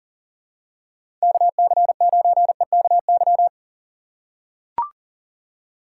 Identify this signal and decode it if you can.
Morse code, heard as a keyed tone, about 40 wpm, 700 Hz. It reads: KC9EKY
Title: USA Call Signs - 6 Characters - 6x Character Spacing - 40wpm